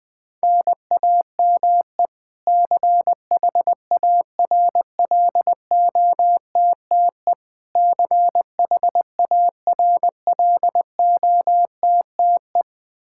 Morse code, heard as a keyed tone, about 20 wpm, 700 Hz. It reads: NAME CHARLOTTE CHARLOTTE
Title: Practice QSO - Exchange 1 - Beginning - 20wpm